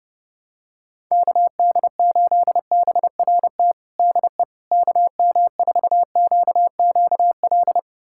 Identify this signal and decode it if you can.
Morse code, heard as a keyed tone, about 30 wpm, 700 Hz. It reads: KD8BRT DE KM4QQL
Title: USA Callsign DE Callsign 30wpm